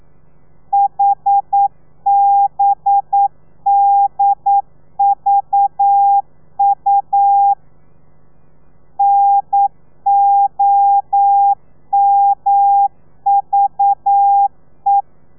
Morse code, heard as a keyed tone, about 9 wpm, 800 Hz. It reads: HBDVU NOMVE
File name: telegrafia / i2tfj / Lez4 40